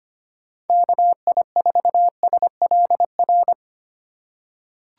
Morse code, heard as a keyed tone, about 25 wpm, 700 Hz. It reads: KI4SLR
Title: USA Call Signs - 6 Characters - 2x Character Spacing - 25wpm